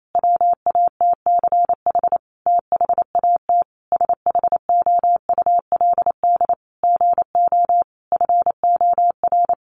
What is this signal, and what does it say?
Morse code, heard as a keyed tone, about 28 wpm, 700 Hz. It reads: WATCH THAT SHOULD GO FOR
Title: Sentences from Top 1000 Words 28wpm